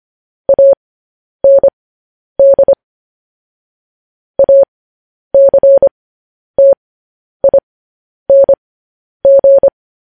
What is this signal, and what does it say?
Morse code, heard as a keyed tone, about 25 wpm, spaced as for 10 wpm, 550 Hz. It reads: AND ACTING
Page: news headlines in morse code at 10 wpm